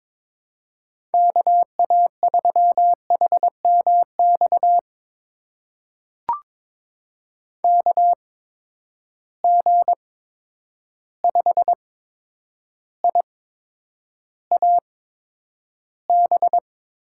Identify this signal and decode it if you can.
Morse code, heard as a keyed tone, about 22 wpm, spaced as for 9 wpm, 700 Hz. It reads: KA3HMX KG5IAB
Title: USA Call Signs - 6 Characters - 8x Character Spacing - 22wpm